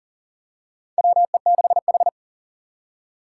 Morse code, heard as a keyed tone, about 37 wpm, 690 Hz. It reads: WE6H
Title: USA Call Signs - 4 Characters - 2x Character Spacing - 40wpm